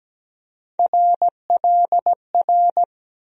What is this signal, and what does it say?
Morse code, heard as a keyed tone, about 17 wpm, 700 Hz. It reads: RLR
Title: USA Call Sign Postfixes - 3 Letters 17wpm